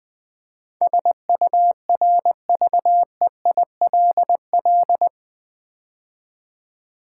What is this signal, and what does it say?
Morse code, heard as a keyed tone, about 20 wpm, 700 Hz. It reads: SURVEILL
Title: Build A Word - 4 Letters At A Time - 12-Letter Words - 20wpm